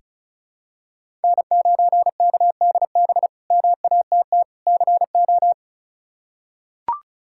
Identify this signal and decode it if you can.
Morse code, heard as a keyed tone, about 34 wpm, 700 Hz. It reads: N9KDB MATT CO
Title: K1USN SST QSO Practice - 35wpm